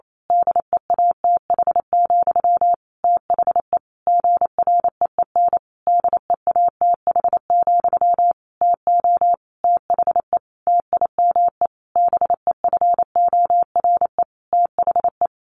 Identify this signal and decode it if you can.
Morse code, heard as a keyed tone, about 28 wpm, 700 Hz. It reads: DEATH, THE GREEN DEATH, TO THE TIME BEFORE THE
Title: A Sound Of Thunder 28wpm